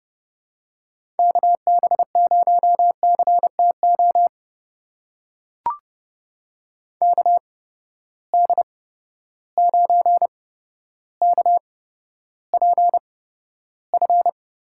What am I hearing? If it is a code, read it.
Morse code, heard as a keyed tone, about 30 wpm, spaced as for 13 wpm, 700 Hz. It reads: KB0CTO KD9KPF
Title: USA Call Signs - 6 Characters - 8x Character Spacing - 30wpm